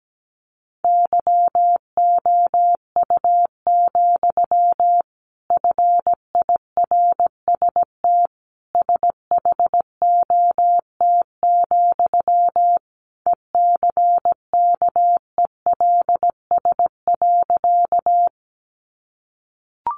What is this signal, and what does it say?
Morse code, heard as a keyed tone, about 17 wpm, 700 Hz. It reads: YOU, FIRST SHOT, ECKELS.
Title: A Sound Of Thunder 17wpm